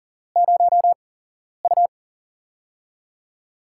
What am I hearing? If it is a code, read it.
Morse code, heard as a keyed tone, about 42 wpm, spaced as for 11 wpm, 700 Hz. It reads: 0U
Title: USA Call Signs - 8x Character Spacing - 40wpm